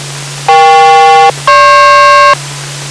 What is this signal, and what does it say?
Aircraft SELCAL, heard as two pairs of tones. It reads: EK-GP